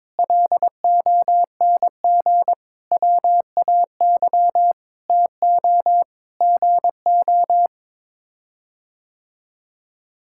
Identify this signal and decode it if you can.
Morse code, heard as a keyed tone, about 22 wpm, 700 Hz. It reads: LONG WAY TO GO
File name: practice / 22wpm / Sets of 7 Words - 22wpm